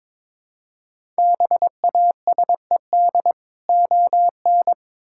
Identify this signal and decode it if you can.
Morse code, heard as a keyed tone, about 22 wpm, 700 Hz. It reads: BASED ON
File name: Build A Phrase - 2-Word Phrases - 22wpm